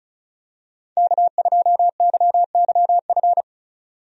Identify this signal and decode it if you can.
Morse code, heard as a keyed tone, about 36 wpm, 700 Hz. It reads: K2YYF
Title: USA Call Signs - 5 Characters - 35wpm